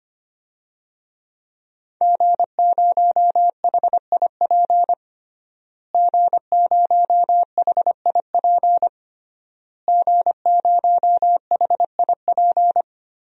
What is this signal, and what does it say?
Morse code, heard as a keyed tone, about 25 wpm, 700 Hz. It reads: G0HIP G0HIP G0HIP
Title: DXCC-100 Call Signs - 5 Characters - Repeated 3x - 25wpm